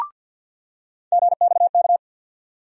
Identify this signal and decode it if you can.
Morse code, heard as a keyed tone, about 48 wpm, 700 Hz. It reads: GXK
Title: USA Call Sign Postfixes - 3 Letters 50wpm